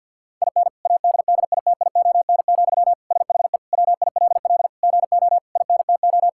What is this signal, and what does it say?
Morse code, heard as a keyed tone, about 50 wpm, 700 Hz. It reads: IN ADDITION, SHE WILL GO INTO
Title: Sentences from Top 1000 Words - Review Part 6 - 50wpm